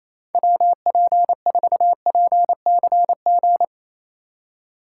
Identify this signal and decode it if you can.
Morse code, heard as a keyed tone, about 36 wpm, 700 Hz. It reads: WP4PCG